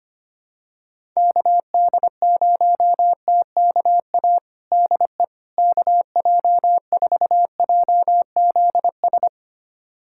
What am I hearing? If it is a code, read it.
Morse code, heard as a keyed tone, about 25 wpm, 700 Hz. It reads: KD0TKA DE KJ4JZS
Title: USA Callsign DE Callsign - 6 Characters - 25wpm